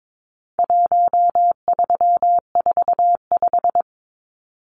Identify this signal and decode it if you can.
Morse code, heard as a keyed tone, about 22 wpm, 700 Hz. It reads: 1345
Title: Numbers - 4-Digit 22wpm